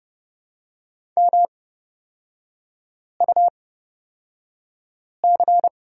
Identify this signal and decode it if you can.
Morse code, heard as a keyed tone, about 30 wpm, spaced as for 5 wpm, 700 Hz. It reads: MUC
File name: Lesson 36e - BT - Three Characters 30wpm